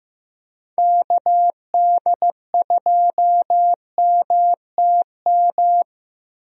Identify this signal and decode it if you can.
Morse code, heard as a keyed tone, about 15 wpm, 700 Hz. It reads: KD2MTM
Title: USA Call Signs - 6 Characters - 15wpm